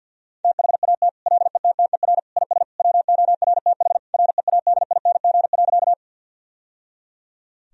Morse code, heard as a keyed tone, about 52 wpm, 700 Hz. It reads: THAT LETTER IS WORTH READING.
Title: Sentences from Top 1000 Words - Review Part 3 - 50wpm